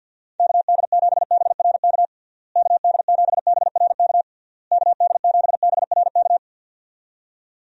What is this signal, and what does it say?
Morse code, heard as a keyed tone, about 52 wpm, 700 Hz. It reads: KD7BRK KD7BRK KD7BRK
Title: USA Call Signs - 6 Characters - Repeated 3x - 50wpm